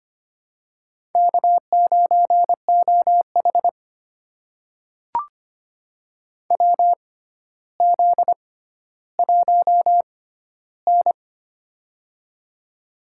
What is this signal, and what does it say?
Morse code, heard as a keyed tone, about 25 wpm, spaced as for 13 wpm, 700 Hz. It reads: K9OH WZ1N
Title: USA Call Signs - 4 Characters - 6x Character Spacing - 25wpm